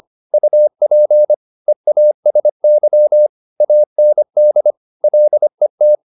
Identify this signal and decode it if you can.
Morse code, heard as a keyed tone, about 25 wpm, 600 Hz. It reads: UP EASY AND LET
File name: the old man and the sea part 1 25wpm